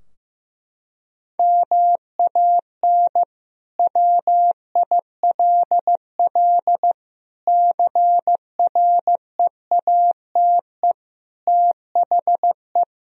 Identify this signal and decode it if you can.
Morse code, heard as a keyed tone, about 16 wpm, 700 Hz. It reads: MAN WILL CREATE THE
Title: Sentences from Top 1000 Words - Review Part 4 - 15wpm